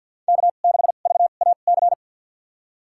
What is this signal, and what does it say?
Morse code, heard as a keyed tone, about 52 wpm, spaced as for 36 wpm, 700 Hz. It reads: K6VAC
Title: USA Call Signs - 5 Characters - 2x Character Spacing - 50wpm